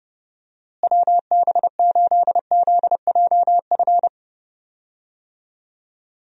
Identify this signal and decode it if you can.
Morse code, heard as a keyed tone, about 32 wpm, 700 Hz. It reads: WB8ZJF
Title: USA Call Signs - 6 Characters - Repeated 3x - 30wpm